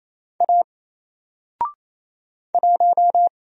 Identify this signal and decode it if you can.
Morse code, heard as a keyed tone, about 27 wpm, 700 Hz. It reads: A 1
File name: Single-Letter-Number-28wpm